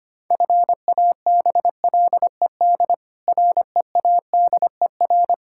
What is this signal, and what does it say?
Morse code, heard as a keyed tone, about 25 wpm, 700 Hz. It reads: FABLED READER
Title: Pangram Challenge v2 - 25wpm